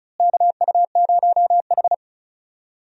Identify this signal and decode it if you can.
Morse code, heard as a keyed tone, about 36 wpm, 700 Hz. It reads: KU0H
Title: USA Call Signs - 4 Characters - Repeated 3x - 35wpm